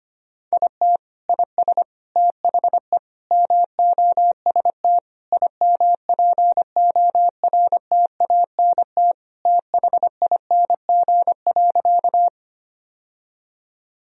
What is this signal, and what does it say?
Morse code, heard as a keyed tone, about 25 wpm, 700 Hz. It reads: IT IS THE MOST IMPORTANT THING.
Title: Sentences from Top 1000 Words - Review Part 1 - 25wpm